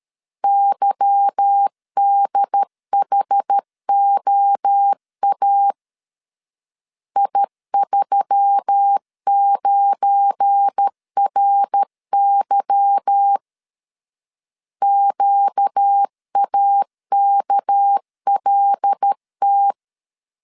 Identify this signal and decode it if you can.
Morse code, heard as a keyed tone, about 13 wpm, 800 Hz. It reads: YDHOA I39RY QAKLT
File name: LEZIONI / 8 YJ45 / 8 Y 12wpm 2228